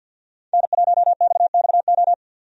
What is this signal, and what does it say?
Morse code, heard as a keyed tone, about 47 wpm, 700 Hz. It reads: N1XXO
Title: USA Call Signs - 5 Characters - 50wpm